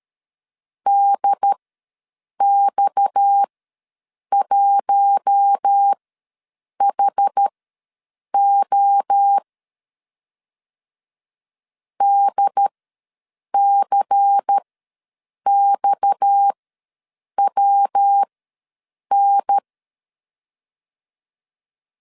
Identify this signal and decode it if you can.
Morse code, heard as a keyed tone, about 13 wpm, spaced as for 7 wpm, 800 Hz. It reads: DX1HO DCXWN
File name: LEZIONI / 6 PKX0 / 6 PKX ws 12wpm 3106